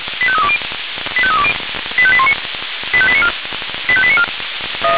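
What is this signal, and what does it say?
Five-tone selective call (ZVEI), heard as quick runs of five tones, tones of about 70 ms. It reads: 85320 85320 85819 85894 85894
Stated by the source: Received on SDR as a ZVEI five-tone selective call, code 85320.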